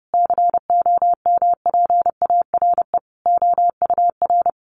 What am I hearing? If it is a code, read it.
Morse code, heard as a keyed tone, about 30 wpm, 700 Hz. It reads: COMPARE OUR
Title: Sentences from Top 1300 Words 30wpm